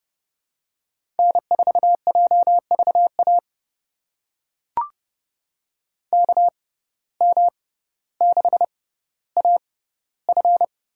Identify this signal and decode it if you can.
Morse code, heard as a keyed tone, about 31 wpm, spaced as for 16 wpm, 700 Hz. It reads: N4JVA KM6AF
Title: USA Call Signs - 5 Characters - 6x Character Spacing - 30wpm